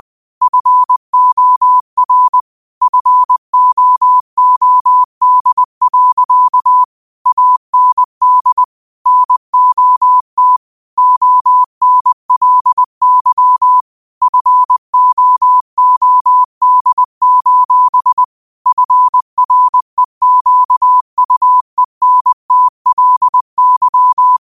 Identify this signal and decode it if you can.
Morse code, heard as a keyed tone, about 20 wpm, 1000 Hz. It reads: FOR FOOD. AND NOT ONLY FOOD: FREQUENTLY